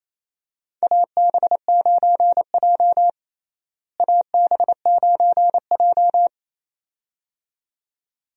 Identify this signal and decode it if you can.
Morse code, heard as a keyed tone, about 28 wpm, 700 Hz. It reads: AB9J AB9J
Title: USA Call Signs - 4 Characters - Repeated 2x - 28wpm